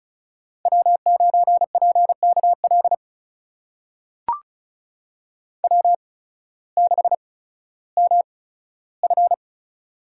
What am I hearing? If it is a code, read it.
Morse code, heard as a keyed tone, about 36 wpm, spaced as for 16 wpm, 700 Hz. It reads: W9PKL W6MF
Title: USA Call Signs - 8x Character Spacing - 35wpm